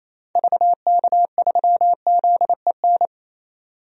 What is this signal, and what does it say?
Morse code, heard as a keyed tone, about 28 wpm, 700 Hz. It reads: VK3ZEN
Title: DXCC-100 Call Signs - 6 Characters - 28wpm